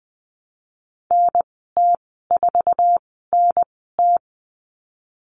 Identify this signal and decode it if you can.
Morse code, heard as a keyed tone, about 20 wpm, spaced as for 15 wpm, 700 Hz. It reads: NT4NT
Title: 055 Lesson 10g - Letter r - Call Signs 20wpm